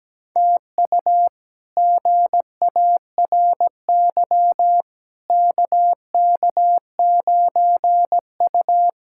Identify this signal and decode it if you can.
Morse code, heard as a keyed tone, about 17 wpm, 700 Hz. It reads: TU GARY KK9U
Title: K1USN SST QSO Practice - 17wpm